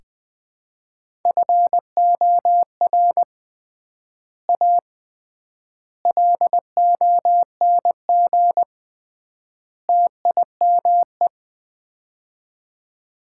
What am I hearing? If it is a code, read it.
Morse code, heard as a keyed tone, about 21 wpm, 700 Hz. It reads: FOR A LONG TIME
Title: 3x Word-Spacing - Sets of 4 Words - 20wpm